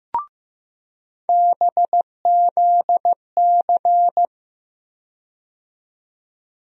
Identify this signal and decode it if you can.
Morse code, heard as a keyed tone, about 15 wpm, 700 Hz. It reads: BZC
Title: USA Call Sign Postfixes - 3 Letters 15wpm